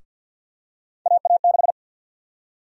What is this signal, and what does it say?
Morse code, heard as a keyed tone, about 51 wpm, 700 Hz. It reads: AA6
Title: USA Call Sign Prefixes - 50wpm